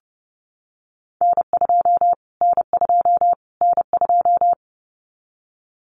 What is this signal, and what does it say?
Morse code, heard as a keyed tone, about 29 wpm, 700 Hz. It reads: N2 N2 N2
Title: Callsign Prefixes for Top 25 DXCC - Repeated 3x - 30wpm